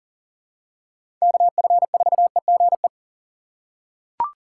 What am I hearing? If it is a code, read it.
Morse code, heard as a keyed tone, about 40 wpm, 700 Hz. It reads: KF4EGE
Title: USA Call Signs - 6 Characters - 6x Character Spacing - 40wpm